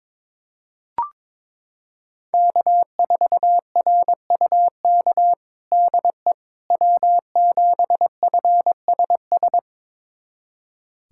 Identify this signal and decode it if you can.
Morse code, heard as a keyed tone, about 22 wpm, 700 Hz. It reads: K4RUK DE W7FSS
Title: USA Callsign DE Callsign - 5 Characters - 22wpm